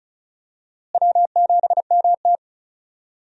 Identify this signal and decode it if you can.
Morse code, heard as a keyed tone, about 36 wpm, 700 Hz. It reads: W7MT
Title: USA Call Signs - 4 Characters - Repeated 2x - 35wpm